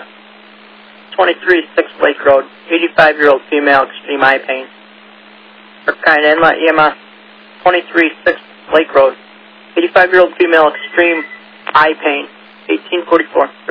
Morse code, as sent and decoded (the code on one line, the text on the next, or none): none